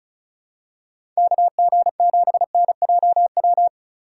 Morse code, heard as a keyed tone, about 34 wpm, 700 Hz. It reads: KG7NJW
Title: USA Call Signs - 6 Characters - 4x Character Spacing - 35wpm